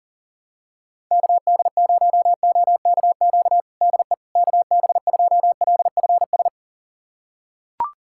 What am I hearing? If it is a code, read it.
Morse code, heard as a keyed tone, about 37 wpm, 700 Hz. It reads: KD0OKQ DE KB2LFS
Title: USA Callsign DE Callsign - 6 Characters - 40wpm